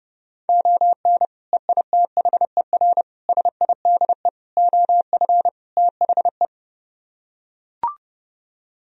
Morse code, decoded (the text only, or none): ON EITHER SIDE OF THE